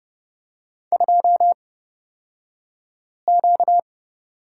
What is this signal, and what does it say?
Morse code, heard as a keyed tone, about 30 wpm, spaced as for 5 wpm, 700 Hz. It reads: 2Q
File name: Lesson 38d - Letter q - Two Characters 30wpm